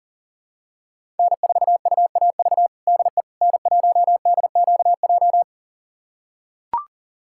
Morse code, heard as a keyed tone, about 40 wpm, 700 Hz. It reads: N4UAV DE N1DQJ
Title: USA Callsign DE Callsign - 5 Characters - 40wpm